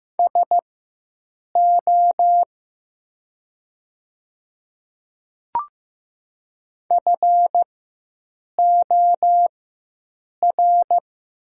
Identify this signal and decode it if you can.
Morse code, heard as a keyed tone, about 15 wpm, spaced as for 7 wpm, 700 Hz. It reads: SO FOR
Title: 4x Character Spacing - Top 100 Words - 15wpm